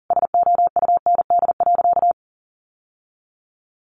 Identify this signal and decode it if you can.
Morse code, heard as a keyed tone, about 40 wpm, 700 Hz. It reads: SOUND.
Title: Sentences from Top 900 Words 40wpm